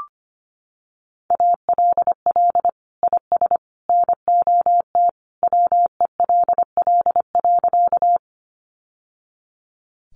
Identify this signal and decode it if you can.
Morse code, heard as a keyed tone, about 25 wpm, 700 Hz. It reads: ALL IS NOT WELL.
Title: Sentences from Top 600 Words 25wpm